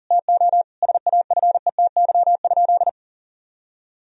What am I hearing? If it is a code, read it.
Morse code, heard as a keyed tone, about 40 wpm, 700 Hz. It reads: TO SAFETY?
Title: Sentences from Top 1000 Words - Review Part 4 - 40wpm